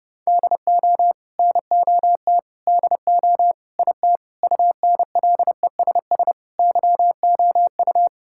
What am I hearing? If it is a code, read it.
Morse code, heard as a keyed tone, about 32 wpm, 700 Hz. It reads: DO NOT DO IT UNLESS YOU
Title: Sentences from Top 1000 Words - Review Part 6 - 30wpm